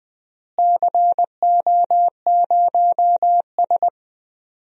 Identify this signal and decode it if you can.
Morse code, heard as a keyed tone, about 20 wpm, 700 Hz. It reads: CO0S
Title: DXCC-100 Call Signs - 4 Characters - Repeated 3x - 20wpm